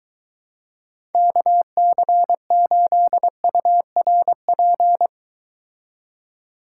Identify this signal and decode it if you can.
Morse code, heard as a keyed tone, about 23 wpm, 700 Hz. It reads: KC8URP